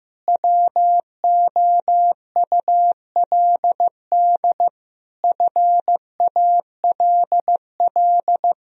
Morse code, heard as a keyed tone, about 15 wpm, 700 Hz. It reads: WOULD FALL
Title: Sentences from Top 1000 Words - Review Part 2 - 15wpm